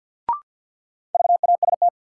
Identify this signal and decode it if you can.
Morse code, heard as a keyed tone, about 48 wpm, 700 Hz. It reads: VAST
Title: Sentences from Top 1300 Words - New Word Drill Alt - 50wpm